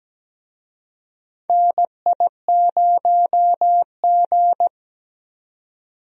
Morse code, heard as a keyed tone, about 17 wpm, 700 Hz. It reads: NI0G
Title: USA Call Signs - 4 Characters - 17wpm